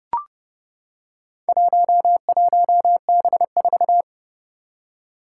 Morse code, heard as a keyed tone, about 30 wpm, 690 Hz. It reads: 11B4
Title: Four Challenging Letter-Number Combinations 30wpm